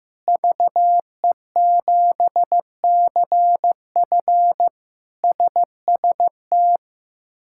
Morse code, heard as a keyed tone, about 15 wpm, 700 Hz. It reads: VE7CF SST
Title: K1USN SST QSO Practice - 15wpm